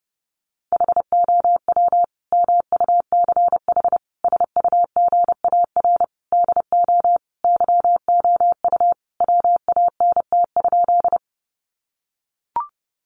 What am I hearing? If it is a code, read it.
Morse code, heard as a keyed tone, about 30 wpm, 700 Hz. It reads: HOW MUCH SUGAR DO YOU WANT?
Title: Sentences from Top 1200 Words 30wpm